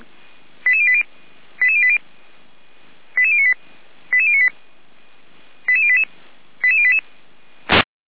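Five-tone selective call (ZVEI), heard as quick runs of five tones, tones of about 70 ms. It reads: 88080 88080 88098 88098 88088 88088
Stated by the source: ZVEI five-tone selective call received by SDR.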